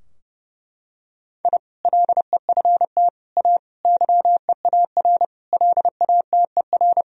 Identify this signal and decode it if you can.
Morse code, heard as a keyed tone, about 30 wpm, 700 Hz. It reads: I LEFT A YEAR LATER
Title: Sentences from Top 1000 Words - Review Part 4 - 30wpm